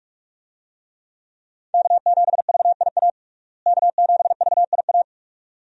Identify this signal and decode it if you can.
Morse code, heard as a keyed tone, about 46 wpm, 700 Hz. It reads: K7VIA K7VIA
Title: USA Call Signs - 5 Characters - Repeated 2x - 45wpm